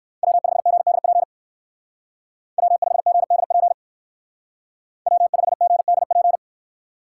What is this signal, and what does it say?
Morse code, heard as a keyed tone, about 58 wpm, 700 Hz. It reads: W5GDP W5GDP W5GDP